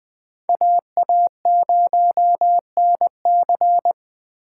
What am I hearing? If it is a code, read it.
Morse code, heard as a keyed tone, about 20 wpm, 700 Hz. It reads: AA0NC